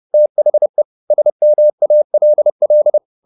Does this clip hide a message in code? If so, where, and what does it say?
Yes: Morse code from 0.1 to 3.0 s, heard as a keyed tone, about 30 wpm, 600 Hz. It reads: THE SMALL